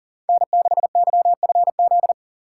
Morse code, heard as a keyed tone, about 37 wpm, 700 Hz. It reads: N6YFZ
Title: USA Call Signs - 5 Characters - 4x Character Spacing - 40wpm